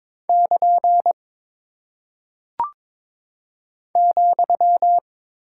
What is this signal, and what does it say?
Morse code, heard as a keyed tone, about 22 wpm, 700 Hz. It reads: <KN> ,
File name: practice / 22wpm / Single Punctuation-Prosign - 22wpm